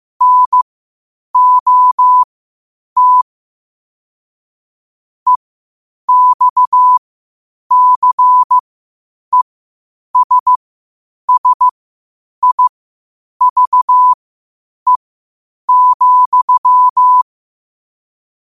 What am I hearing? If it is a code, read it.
Morse code, heard as a keyed tone, about 15 wpm, spaced as for 8 wpm, 1000 Hz. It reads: NOT EXCESSIVE,
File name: qotdfiles / 8wpm